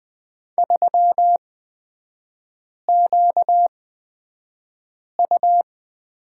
Morse code, heard as a keyed tone, about 20 wpm, spaced as for 5 wpm, 700 Hz. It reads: 3QU
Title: 255 Lesson 41e - Period - Three Characters 20wpm